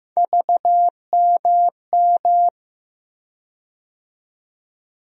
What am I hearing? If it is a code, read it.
Morse code, heard as a keyed tone, about 15 wpm, 700 Hz. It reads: VMM
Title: USA Call Sign Postfixes - 3 Letters 15wpm